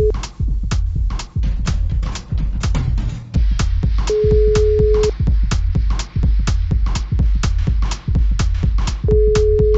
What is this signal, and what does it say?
Ringback tone (Europe, 425 Hz).